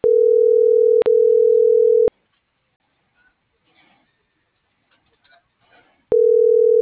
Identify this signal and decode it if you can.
Ringback tone (North America), 440 + 480 Hz.